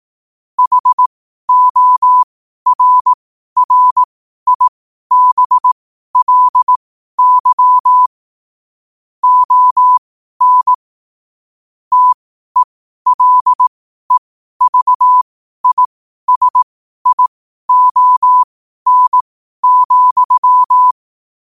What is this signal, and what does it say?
Morse code, heard as a keyed tone, about 18 wpm, spaced as for 12 wpm, 1000 Hz. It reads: HORRIBLY ON TELEVISION,
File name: qotdfiles / 12wpm